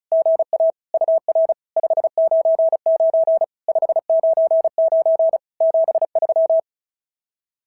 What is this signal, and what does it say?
Morse code, heard as a keyed tone, about 36 wpm, 660 Hz. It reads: GA UR 599 599 73
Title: SOTA QSO Practice 35wpm